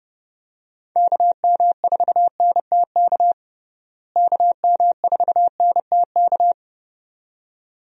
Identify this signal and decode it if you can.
Morse code, heard as a keyed tone, about 30 wpm, 700 Hz. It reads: KM4NTK KM4NTK
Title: USA Call Signs - 6 Characters - Repeated 2x - 30wpm